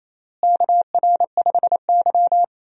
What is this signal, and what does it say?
Morse code, heard as a keyed tone, about 28 wpm, 700 Hz. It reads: KR5Y
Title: USA Call Signs - 4 Characters - Repeated 2x - 28wpm